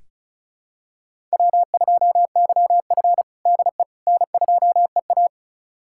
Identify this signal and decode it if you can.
Morse code, heard as a keyed tone, about 36 wpm, 700 Hz. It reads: W2YF DE N2EA
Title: USA Callsign DE Callsign - 4 Characters - 35wpm